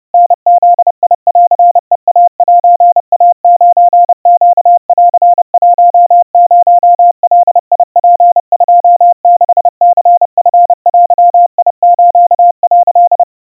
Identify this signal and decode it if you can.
Morse code (Japanese Wabun code), heard as a keyed tone, about 30 wpm, 700 Hz. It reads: タブンヘイセイ9ネン10ガツ26ニチデス」
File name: 110 cpm 700 Hz Crispy